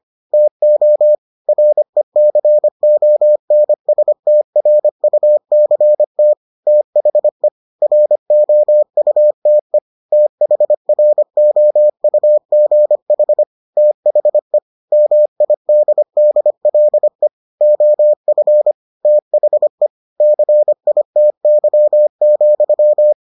Morse code, heard as a keyed tone, about 25 wpm, 600 Hz. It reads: TO RECONSTRUCT THE ROUTE THROUGH THE MIDDLE OF THE CITY,